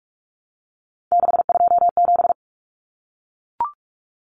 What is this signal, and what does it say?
Morse code, heard as a keyed tone, about 44 wpm, 700 Hz. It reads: 627
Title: Common Cut Numbers - 3-Digit 45wpm